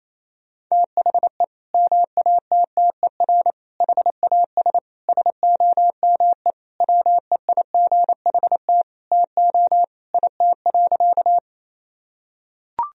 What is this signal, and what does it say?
Morse code, heard as a keyed tone, about 28 wpm, 700 Hz. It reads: THE MATTER HAS SOME WEIGHT TO IT.
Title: Sentences from Top 1000 Words - Review Part 3 - 28wpm